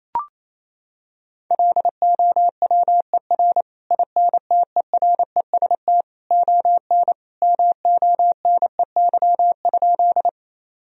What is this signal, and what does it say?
Morse code, heard as a keyed tone, about 28 wpm, 700 Hz. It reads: LOWER INTEREST ON MONEY?
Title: Sentences from Top 1000 Words - Review Part 4 - 28wpm